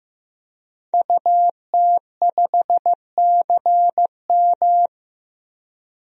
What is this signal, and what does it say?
Morse code, heard as a keyed tone, about 15 wpm, 700 Hz. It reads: UT5CM
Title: DXCC-100 Call Signs - 5 Characters - 15wpm